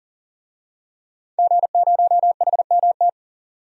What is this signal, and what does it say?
Morse code, heard as a keyed tone, about 40 wpm, 700 Hz. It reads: G0HMT
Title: DXCC-100 Call Signs - 5 Characters - 40wpm